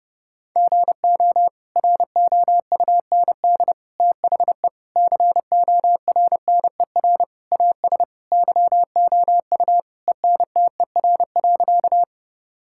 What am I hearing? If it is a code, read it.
Morse code, heard as a keyed tone, about 30 wpm, 700 Hz. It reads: GO ROUND THE CORNER AS YOU ENTER.
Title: Sentences from Top 1600 Words 30wpm